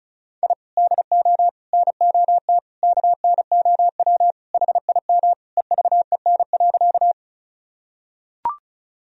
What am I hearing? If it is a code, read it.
Morse code, heard as a keyed tone, about 36 wpm, 700 Hz. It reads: I DO NOT KNOW HIM EVEN.
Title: Sentences from Top 1000 Words - Review Part 1 - 35wpm